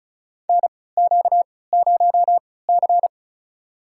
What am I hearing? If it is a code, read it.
Morse code, heard as a keyed tone, about 36 wpm, spaced as for 20 wpm, 700 Hz. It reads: NQ0C
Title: USA Call Signs - 3x Character Spacing - 35wpm